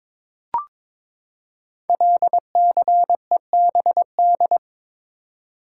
Morse code, heard as a keyed tone, about 22 wpm, 700 Hz. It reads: LCEBD
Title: Code Groups - 5 Letters - 22wpm